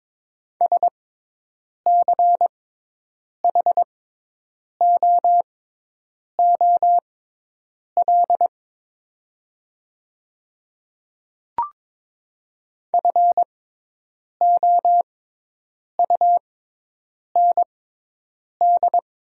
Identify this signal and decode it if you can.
Morse code, heard as a keyed tone, about 22 wpm, spaced as for 8 wpm, 700 Hz. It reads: SCHOOL FOUND
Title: 6x Character Spacing - Top 300 Words - 22wpm